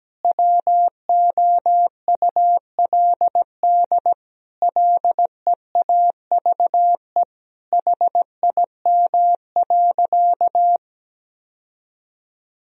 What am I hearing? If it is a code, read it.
Morse code, heard as a keyed tone, about 17 wpm, 700 Hz. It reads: WOULD LEAVE HIM.
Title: Sentences from Top 1000 Words - Review Part 4 - 17wpm